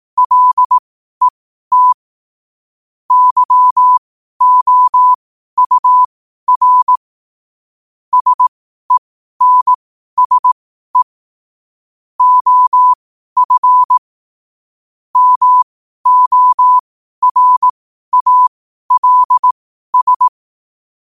Morse code, heard as a keyed tone, about 18 wpm, spaced as for 12 wpm, 1000 Hz. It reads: LET YOUR SENSE OF MORALS
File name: qotdfiles / 12wpm